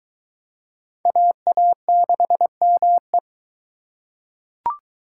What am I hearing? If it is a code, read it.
Morse code, heard as a keyed tone, about 22 wpm, 700 Hz. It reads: AA6ME